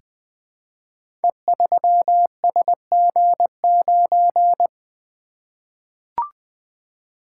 Morse code, heard as a keyed tone, about 20 wpm, 700 Hz. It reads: E3SG9
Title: Code Groups - 5 Letter:Numbers - 20wpm